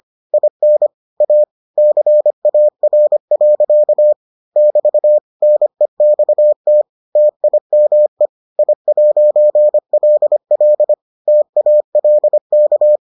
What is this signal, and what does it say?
Morse code, heard as a keyed tone, about 25 wpm, 600 Hz. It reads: IN A CAR. <BT> NEXT TIME I'LL TALK